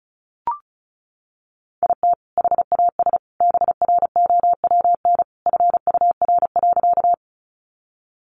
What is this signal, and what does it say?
Morse code, heard as a keyed tone, about 34 wpm, 700 Hz. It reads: IT HAS BROWN FUR.
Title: Easy Phrases 35wpm